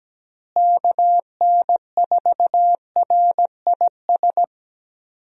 Morse code, heard as a keyed tone, about 17 wpm, 700 Hz. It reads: KN4RIS
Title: USA Call Signs - 6 Characters - Repeated 3x - 17wpm